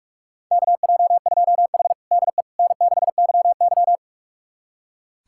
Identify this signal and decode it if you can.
Morse code, heard as a keyed tone, about 46 wpm, 700 Hz. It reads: KJ2H DE N6YY
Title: USA Callsign DE Callsign - 4 Characters - 45wpm